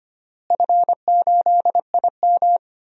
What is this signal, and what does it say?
Morse code, heard as a keyed tone, about 25 wpm, 700 Hz. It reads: F8IM
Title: DXCC-100 Call Signs - 4 Characters - Repeated 3x - 25wpm